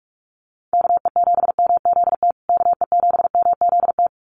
Morse code, heard as a keyed tone, about 44 wpm, 700 Hz. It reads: KE7MZT KE7MZT
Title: USA Call Signs Repeated 2x 45wpm